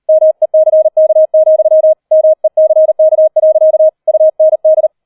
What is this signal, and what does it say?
Morse code, heard as a keyed tone, about 38 wpm, 630 Hz. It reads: MECK, MECK. UND